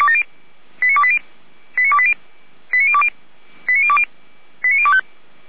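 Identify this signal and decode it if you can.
Five-tone selective call (ZVEI), heard as quick runs of five tones, tones of about 70 ms. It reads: 89280 89280 89020 89020 89025
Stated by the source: ZVEI five-tone call received by SDR, recorded by Feuerwehr Orlamünde.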